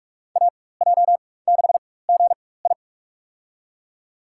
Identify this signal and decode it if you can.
Morse code, heard as a keyed tone, about 44 wpm, spaced as for 21 wpm, 690 Hz. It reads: AJ6GI
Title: USA Call Signs - 4x Character Spacing - 45wpm